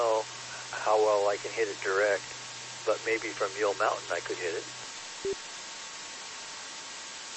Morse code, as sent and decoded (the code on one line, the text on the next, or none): none